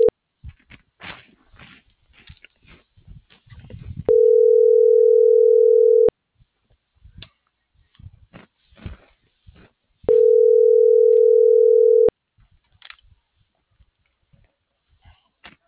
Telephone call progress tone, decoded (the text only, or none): ringback tone